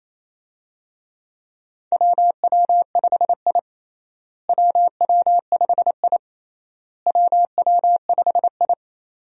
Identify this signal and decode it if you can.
Morse code, heard as a keyed tone, about 28 wpm, 700 Hz. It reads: WW5I WW5I WW5I
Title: USA Call Signs - 4 Characters - Repeated 3x - 28wpm